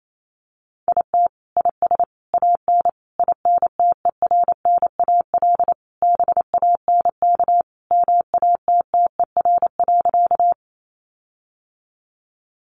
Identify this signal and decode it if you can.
Morse code, heard as a keyed tone, about 28 wpm, 700 Hz. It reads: IT IS AN INTERNAL BANK MATTER.
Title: Sentences from Top 1400 Words 28wpm